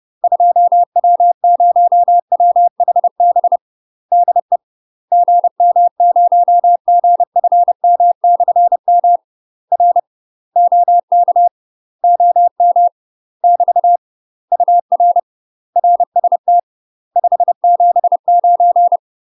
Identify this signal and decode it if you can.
Morse code, heard as a keyed tone, about 30 wpm, 700 Hz. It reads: 2W0WHB DE GM0GFM/M R OK OM <BT> UR RST 579
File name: UK MASTER QSOs TRAINING/30WPM